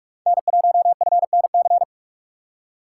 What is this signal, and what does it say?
Morse code, heard as a keyed tone, about 48 wpm, 690 Hz. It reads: N1FNC
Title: USA Call Signs - 5 Characters - 2x Character Spacing - 45wpm